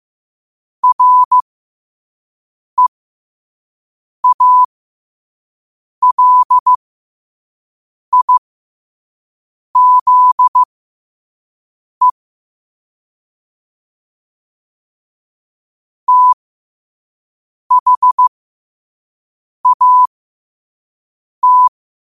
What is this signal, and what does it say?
Morse code, heard as a keyed tone, about 15 wpm, spaced as for 5 wpm, 1000 Hz. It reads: REALIZE THAT
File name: qotdfiles / 5wpm